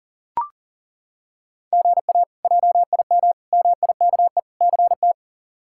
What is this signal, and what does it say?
Morse code, heard as a keyed tone, about 40 wpm, 700 Hz. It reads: GA JIM MIKE CT
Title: K1USN SST QSO Practice - 40wpm